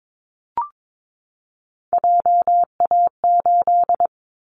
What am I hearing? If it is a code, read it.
Morse code, heard as a keyed tone, about 22 wpm, 700 Hz. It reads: JA8
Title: Callsign Prefixes for Top 50 DXCC - 22wpm